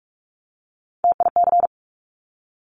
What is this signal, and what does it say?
Morse code, heard as a keyed tone, about 46 wpm, 700 Hz. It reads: TIC
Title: Top Three Letter Combinations in Words 45wpm